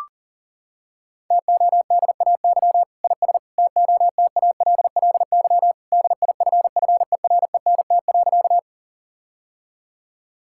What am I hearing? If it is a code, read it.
Morse code, heard as a keyed tone, about 37 wpm, 700 Hz. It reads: TODAY IS TOTALLY DIFFERENT.
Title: Sentences from Top 1600 Words 40wpm